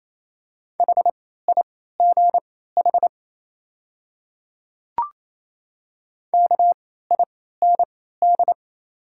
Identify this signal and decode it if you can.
Morse code, heard as a keyed tone, about 28 wpm, spaced as for 16 wpm, 700 Hz. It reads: HIGH KIND
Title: 3x Character Spacing - Top 200 Words - 28wpm